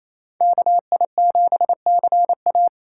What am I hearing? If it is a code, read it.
Morse code, heard as a keyed tone, about 35 wpm, 700 Hz. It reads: KI7CA
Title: Speed Racing - USA Call Signs 28wpm